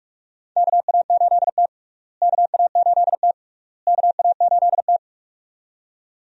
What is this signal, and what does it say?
Morse code, heard as a keyed tone, about 46 wpm, 700 Hz. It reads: KA8T KA8T KA8T
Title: USA Call Signs - 4 Characters - Repeated 3x - 45wpm